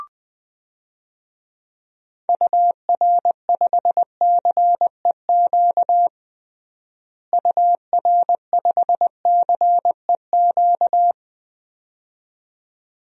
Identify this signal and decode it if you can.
Morse code, heard as a keyed tone, about 20 wpm, 700 Hz. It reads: UR5CEQ UR5CEQ
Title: DXCC-100 Call Signs - 6 Characters - Repeated 2x - 20wpm